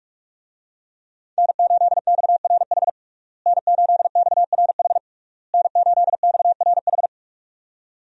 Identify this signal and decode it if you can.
Morse code, heard as a keyed tone, about 44 wpm, 700 Hz. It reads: N8XRH N8XRH N8XRH
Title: USA Call Signs - 5 Characters - Repeated 3x - 45wpm